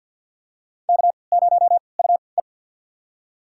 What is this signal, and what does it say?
Morse code, heard as a keyed tone, about 50 wpm, spaced as for 28 wpm, 700 Hz. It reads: K0UE
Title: USA Call Signs - 4 Characters - 3x Character Spacing - 50wpm